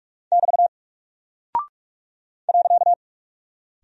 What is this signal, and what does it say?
Morse code, heard as a keyed tone, about 46 wpm, 700 Hz. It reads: <BT> .
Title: Single Character (Letter:Number:Punctuation:Prosign) Alt 45wpm